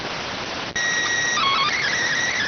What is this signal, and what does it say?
SSTV (slow-scan television), PD 290.